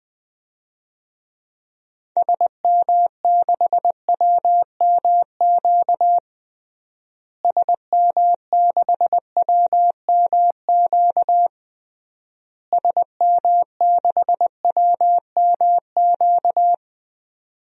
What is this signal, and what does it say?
Morse code, heard as a keyed tone, about 21 wpm, 700 Hz. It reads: SM6WMQ SM6WMQ SM6WMQ
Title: DXCC-100 Call Signs - 6 Characters - Repeated 3x - 20wpm